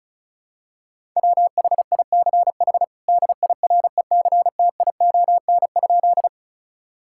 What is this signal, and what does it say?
Morse code, heard as a keyed tone, about 36 wpm, 700 Hz. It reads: WHICH DIRECTION?
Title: Sentences from Top 1000 Words - Review Part 5 - 35wpm